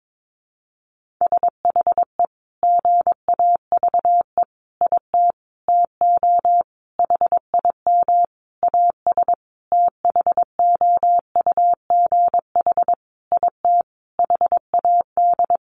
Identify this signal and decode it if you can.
Morse code, heard as a keyed tone, about 22 wpm, 700 Hz. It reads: SHE GAVE IT TO HIM AS THOUGH IT HAD
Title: Sentences from Top 1500 Words 22wpm